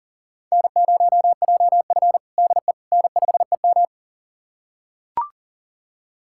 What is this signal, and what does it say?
Morse code, heard as a keyed tone, about 40 wpm, 700 Hz. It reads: N0JF DE N5EM
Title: USA Callsign DE Callsign - 4 Characters - 40wpm